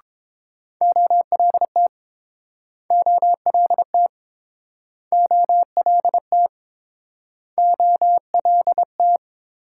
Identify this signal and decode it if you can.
Morse code, heard as a keyed tone, about 30 wpm, 700 Hz. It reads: OLT OLT OLT OLT